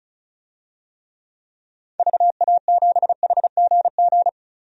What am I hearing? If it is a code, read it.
Morse code, heard as a keyed tone, about 36 wpm, 700 Hz. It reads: VA7HGG
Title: DXCC-100 Call Signs - 6 Characters - 35wpm